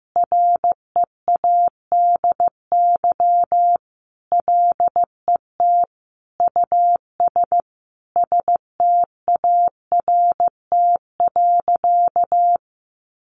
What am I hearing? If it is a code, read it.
Morse code, heard as a keyed tone, about 15 wpm, 700 Hz. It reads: READY LET US START.
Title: Sentences from Top 700 Words 15wpm